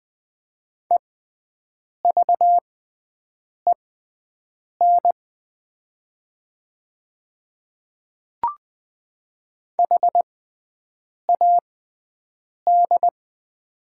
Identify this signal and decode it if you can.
Morse code, heard as a keyed tone, about 20 wpm, spaced as for 7 wpm, 700 Hz. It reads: EVEN HAD
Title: 6x Character Spacing - Top 200 Words - 20wpm